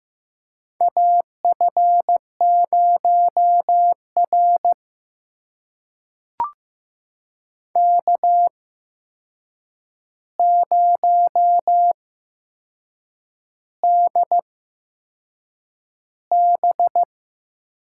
Morse code, heard as a keyed tone, about 15 wpm, spaced as for 6 wpm, 700 Hz. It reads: AF0R K0DB
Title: USA Call Signs - 4 Characters - 8x Character Spacing - 15wpm